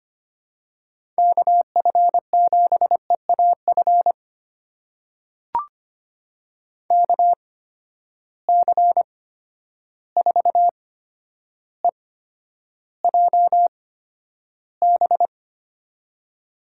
Morse code, heard as a keyed tone, about 25 wpm, spaced as for 11 wpm, 700 Hz. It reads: KF7EAF KC4EJB
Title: USA Call Signs - 6 Characters - 8x Character Spacing - 25wpm